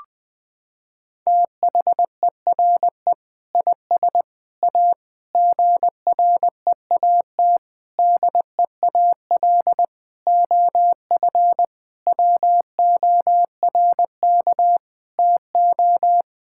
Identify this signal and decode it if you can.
Morse code, heard as a keyed tone, about 21 wpm, 700 Hz. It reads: THERE IS A GREAT DEAL OF WORK TO
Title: Sentences with 6-Word Phrases - 20wpm